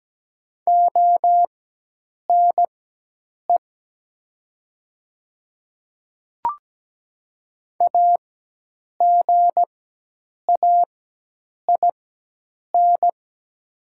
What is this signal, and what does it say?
Morse code, heard as a keyed tone, about 17 wpm, spaced as for 8 wpm, 700 Hz. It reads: ONE AGAIN
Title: 4x Character Spacing - Top 200 Words - 17wpm